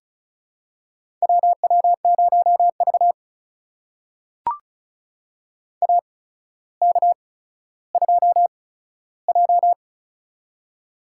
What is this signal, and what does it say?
Morse code, heard as a keyed tone, about 36 wpm, spaced as for 15 wpm, 700 Hz. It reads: WW0V AK2J
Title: USA Call Signs - 4 Characters - 8x Character Spacing - 35wpm